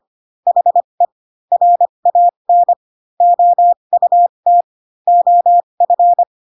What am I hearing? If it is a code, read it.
Morse code, heard as a keyed tone, about 25 wpm, spaced as for 22 wpm, 700 Hz. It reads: HE RAN OUT OF